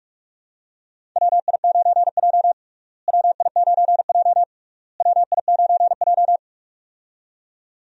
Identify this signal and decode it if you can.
Morse code, heard as a keyed tone, about 46 wpm, 700 Hz. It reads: WI9J WI9J WI9J
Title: USA Call Signs - 4 Characters - Repeated 3x - 45wpm